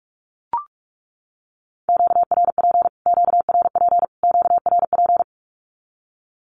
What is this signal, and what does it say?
Morse code, heard as a keyed tone, about 44 wpm, 700 Hz. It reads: QRP QRP QRP
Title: Q-Codes Repeated 3x 45wpm